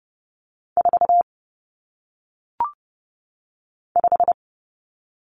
Morse code, heard as a keyed tone, about 30 wpm, 700 Hz. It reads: 4 5
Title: Numbers - 1-Digit 30wpm